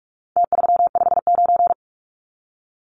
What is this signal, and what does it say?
Morse code, heard as a keyed tone, about 44 wpm, 700 Hz. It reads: T359
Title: Numbers - Contest Style - 4-Digit 45wpm